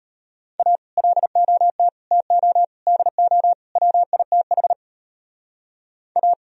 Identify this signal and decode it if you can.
Morse code, heard as a keyed tone, about 31 wpm, 700 Hz. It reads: A LOT TO DO WITH A